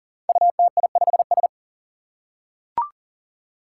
Morse code, heard as a keyed tone, about 40 wpm, 700 Hz. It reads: UTI5S
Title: Five Challenging Letter-Number Combinations 40wpm